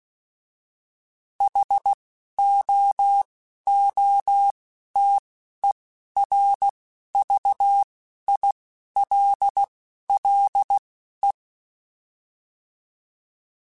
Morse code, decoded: HOOTERVILLE

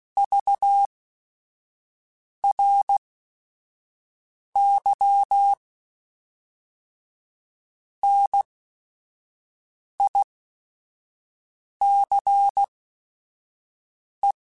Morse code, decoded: VRY NICE

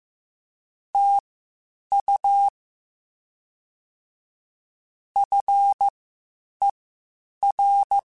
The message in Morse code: TU FER

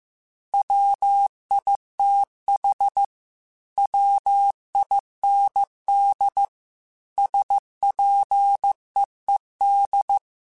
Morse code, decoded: WITH WIND SPEED

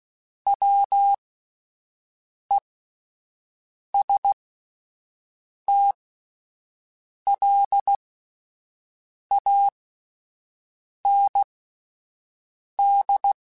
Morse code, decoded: WESTLAND

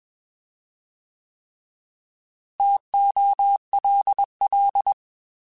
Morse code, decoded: TOLL

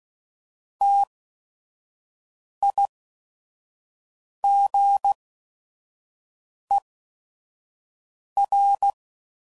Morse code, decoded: TIGER

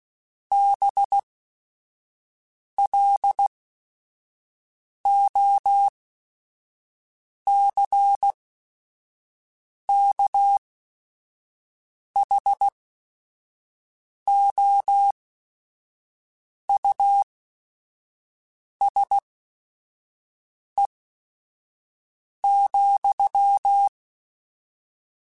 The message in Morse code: BLOCKHOUSE,

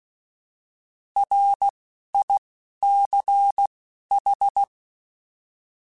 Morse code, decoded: R I C H

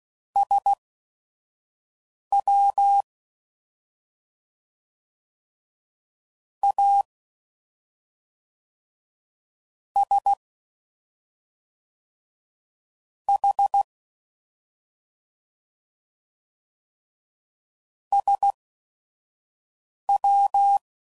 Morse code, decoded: SWASH SW

